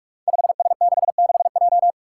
Morse code, heard as a keyed tone, about 44 wpm, 700 Hz. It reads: 5S66J
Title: Five Challenging Letter-Number Combinations 45wpm